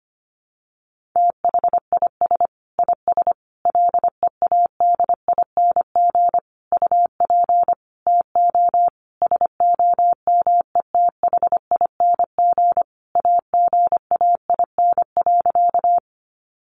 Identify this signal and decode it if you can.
Morse code, heard as a keyed tone, about 25 wpm, 700 Hz. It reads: THIS IS LEADING UP TO SOMETHING AGAIN.
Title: Sentences from Top 1100 Words 25wpm